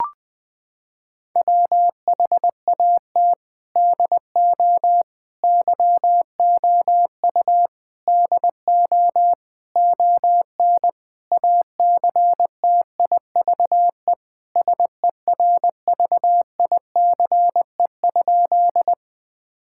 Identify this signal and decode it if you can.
Morse code, heard as a keyed tone, about 20 wpm, 700 Hz. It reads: WHAT DO YOU DO ON ACTIVE SERVICE?
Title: Sentences from Top 1000 Words - Review Part 6 - 20wpm